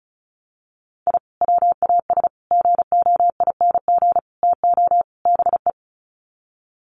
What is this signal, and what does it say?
Morse code, heard as a keyed tone, about 34 wpm, 700 Hz. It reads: I WAS GOING TO BE
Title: Sets of 5 Words 35wpm